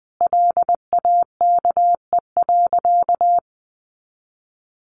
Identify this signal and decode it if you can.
Morse code, heard as a keyed tone, about 20 wpm, 700 Hz. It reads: LAKE.
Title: Sentences from Top 1100 Words 20wpm